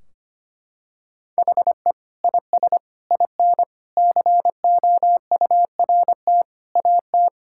Morse code, heard as a keyed tone, about 25 wpm, 700 Hz. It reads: HE IS IN COURT AT
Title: Sentences from Top 1600 Words 25wpm